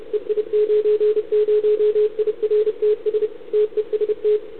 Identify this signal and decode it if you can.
Morse code, heard as a keyed tone, about 33 wpm, 440 Hz. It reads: EI90IRTS TEST
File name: CQ160 2022 CW / SP